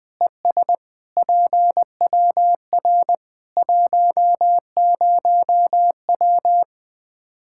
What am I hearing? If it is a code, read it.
Morse code, heard as a keyed tone, about 20 wpm, 700 Hz. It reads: ES PWR 10W
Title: K1USN SST QSO Practice 20wpm